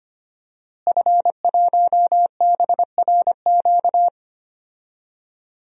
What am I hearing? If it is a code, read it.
Morse code, heard as a keyed tone, about 25 wpm, 700 Hz. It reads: F1BRQ
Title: DXCC-100 Call Signs - 5 Characters - Repeated 3x - 25wpm